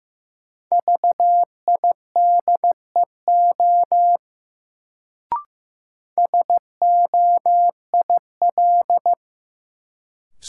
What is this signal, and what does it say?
Morse code, heard as a keyed tone, about 15 wpm, 700 Hz. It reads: VIDEO SOIL
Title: Sentences from Top 1000 Words - Review Part 4 - New Word Drill - 15wpm